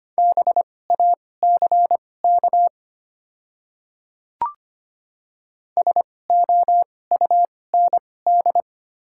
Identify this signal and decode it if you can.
Morse code, heard as a keyed tone, about 25 wpm, spaced as for 18 wpm, 700 Hz. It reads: BACK SOUND
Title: 2x Character Spacing - Top 200 Words - 25wpm